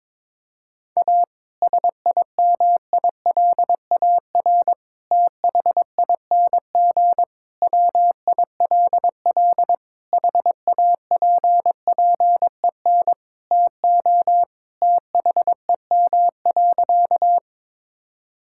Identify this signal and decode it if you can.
Morse code, heard as a keyed tone, about 22 wpm, 700 Hz. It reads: A SIMILAR THING WILL HAPPEN TO THEM.
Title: Sentences from Top 1000 Words - Review Part 5 - 22wpm